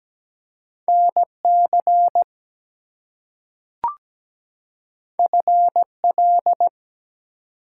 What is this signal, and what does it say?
Morse code, heard as a keyed tone, about 17 wpm, 700 Hz. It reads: NC FL
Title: US States and Canadian Provinces 17wpm